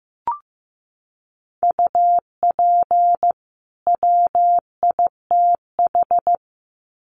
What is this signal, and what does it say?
Morse code, heard as a keyed tone, about 15 wpm, 700 Hz. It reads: UP WITH
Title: Sets of 2 Words 15wpm